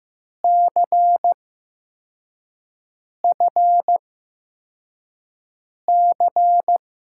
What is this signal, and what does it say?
Morse code, heard as a keyed tone, about 15 wpm, spaced as for 4 wpm, 700 Hz. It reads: CFC
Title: USA Call Signs - 6 Characters - 8x Character Spacing - 15wpm